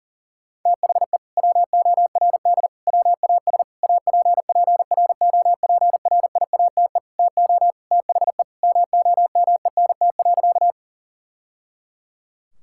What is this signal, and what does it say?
Morse code, heard as a keyed tone, about 40 wpm, 700 Hz. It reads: THE WORD WAS APPROPRIATE TO THE MOMENT.
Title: Sentences from Top 1000 Words - Review Part 5 - 40wpm